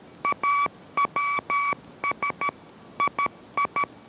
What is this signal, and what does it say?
Morse code, heard as a keyed tone, about 13 wpm, 1140 Hz. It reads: AWS II